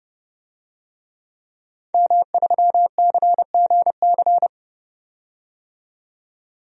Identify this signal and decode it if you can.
Morse code, heard as a keyed tone, about 29 wpm, 700 Hz. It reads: M3CGC
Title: DXCC-100 Call Signs - 5 Characters - 30wpm